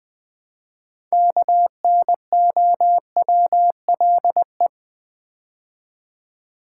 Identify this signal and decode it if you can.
Morse code, heard as a keyed tone, about 20 wpm, 700 Hz. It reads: KNOWLE
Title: Build A Word - 3 Letters At A Time - 9-Letter Words - 20wpm